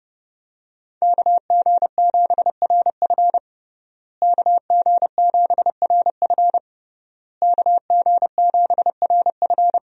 Morse code, heard as a keyed tone, about 32 wpm, 700 Hz. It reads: KG7RF KG7RF KG7RF
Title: USA Call Signs - 5 Characters - Repeated 3x - 30wpm